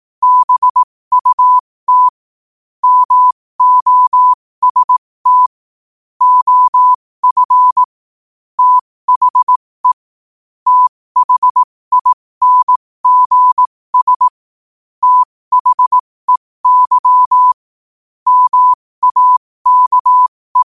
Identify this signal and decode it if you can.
Morse code, heard as a keyed tone, about 18 wpm, spaced as for 15 wpm, 1000 Hz. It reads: BUT MOST OF THE THINGS THEY MAKE
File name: qotdfiles / 15wpm